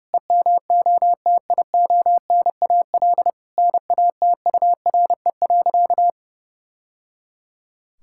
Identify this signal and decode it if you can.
Morse code, heard as a keyed tone, about 30 wpm, 700 Hz. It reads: EMOTIONAL NATURE.
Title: Sentences from Top 1000 Words - Review Part 5 - 30wpm